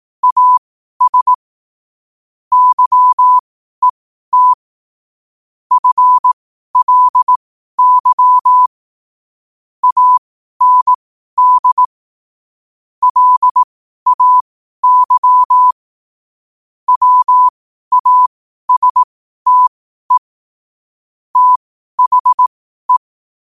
Morse code, heard as a keyed tone, about 18 wpm, spaced as for 12 wpm, 1000 Hz. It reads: AS YET FLY AND LAY WASTE THE
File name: qotdfiles / 12wpm